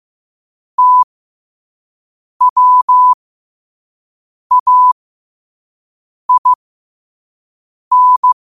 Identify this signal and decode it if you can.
Morse code, heard as a keyed tone, about 15 wpm, spaced as for 5 wpm, 1000 Hz. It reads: TWAIN
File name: qotdfiles / 5wpm